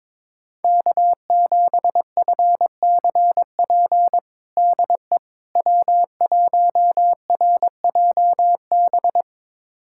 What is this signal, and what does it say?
Morse code, heard as a keyed tone, about 22 wpm, 700 Hz. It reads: K7FCP DE W1RJB
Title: USA Callsign DE Callsign - 5 Characters - 22wpm